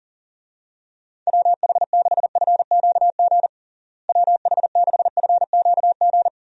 Morse code, heard as a keyed tone, about 40 wpm, 700 Hz. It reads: WH6FQG WH6FQG
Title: USA Call Signs - 6 Characters - Repeated 3x - 40wpm